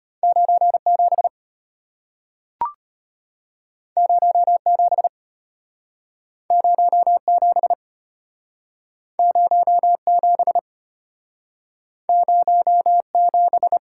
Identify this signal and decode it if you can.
Morse code, heard as a keyed tone, about 34 wpm, 700 Hz. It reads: 97 07 07 07 07